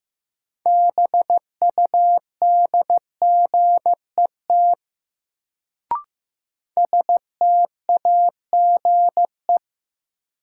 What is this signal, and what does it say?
Morse code, heard as a keyed tone, about 15 wpm, 700 Hz. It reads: BUDGET STAGE
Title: Sentences from Top 1200 Words - New Word Drill - 15wpm